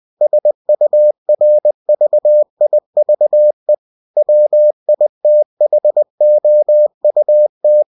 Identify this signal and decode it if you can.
Morse code, heard as a keyed tone, about 20 wpm, 600 Hz. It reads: SURVIVE WITHOUT